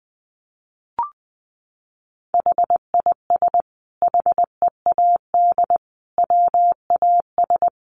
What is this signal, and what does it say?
Morse code, heard as a keyed tone, about 20 wpm, 700 Hz. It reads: HIS HEAD WAS
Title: Sentences from Top 1100 Words 20wpm